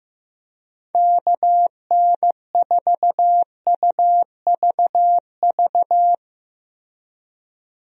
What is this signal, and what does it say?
Morse code, heard as a keyed tone, about 15 wpm, 700 Hz. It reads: KN4UVV
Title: USA Call Signs - 6 Characters - Repeated 2x - 15wpm